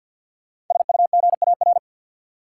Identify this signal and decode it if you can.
Morse code, heard as a keyed tone, about 53 wpm, 700 Hz. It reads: SUGAR
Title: Sentences from Top 1200 Words - New Word Drill - 50wpm